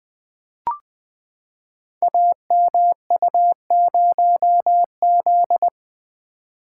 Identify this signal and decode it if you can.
Morse code, heard as a keyed tone, about 20 wpm, 700 Hz. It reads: AMU0Z
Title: Code Groups - 5 Letter:Numbers - 20wpm